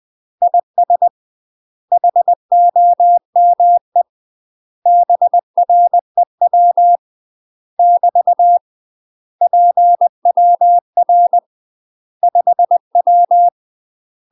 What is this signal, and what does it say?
Morse code, heard as a keyed tone, about 20 wpm, 700 Hz. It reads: IS HOME BREW <BT> PWR 5W
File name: UK MASTER QSOs TRAINING / 20WPM / UK MASTER QSOs 0041